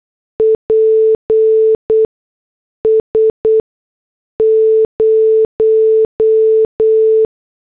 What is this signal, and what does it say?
Morse code, heard as a keyed tone, about 8 wpm, spaced as for 6 wpm, 440 Hz. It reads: PS 0